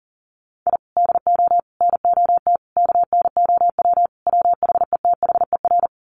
Morse code, heard as a keyed tone, about 40 wpm, 700 Hz. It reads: I DO NOT KNOW WHETHER
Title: Sets of 5 Words 40wpm